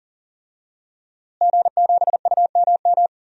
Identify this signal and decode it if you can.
Morse code, heard as a keyed tone, about 40 wpm, 700 Hz. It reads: G7UMM
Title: DXCC-100 Call Signs - 5 Characters - Repeated 2x - 40wpm